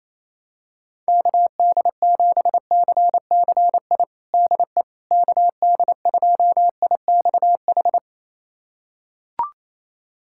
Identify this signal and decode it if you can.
Morse code, heard as a keyed tone, about 28 wpm, 700 Hz. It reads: KD7CCI DE KD2IXH
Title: USA Callsign DE Callsign - 6 Characters - 28wpm